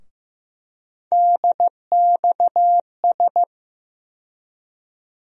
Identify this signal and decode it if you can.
Morse code, heard as a keyed tone, about 15 wpm, 700 Hz. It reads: DXS
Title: USA Call Sign Postfixes - 3 Letters 15wpm